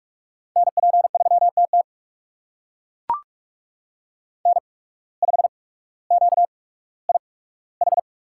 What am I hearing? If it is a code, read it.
Morse code, heard as a keyed tone, about 42 wpm, spaced as for 19 wpm, 700 Hz. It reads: NP3TT N5QIH
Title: USA Call Signs - 5 Characters - 8x Character Spacing - 45wpm